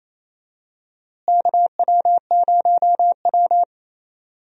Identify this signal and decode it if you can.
Morse code, heard as a keyed tone, about 28 wpm, 700 Hz. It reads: KW0W
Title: USA Call Signs - 4 Characters - Repeated 2x - 28wpm